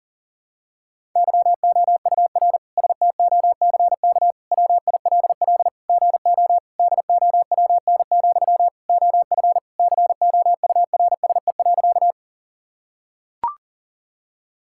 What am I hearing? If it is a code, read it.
Morse code, heard as a keyed tone, about 40 wpm, 700 Hz. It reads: YOUR STOCK WILL GO DOWN, OF COURSE.
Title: Sentences from Top 1000 Words - Review Part 5 - 40wpm